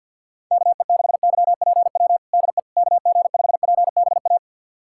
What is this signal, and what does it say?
Morse code, heard as a keyed tone, about 51 wpm, 690 Hz. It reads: KE6YPW DE KG5PBA
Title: USA Callsign DE Callsign - 6 Characters - 50wpm